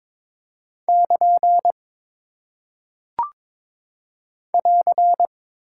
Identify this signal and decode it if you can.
Morse code, heard as a keyed tone, about 22 wpm, 700 Hz. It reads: <KN> <AR>
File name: practice / 22wpm / Single Punctuation-Prosign - 22wpm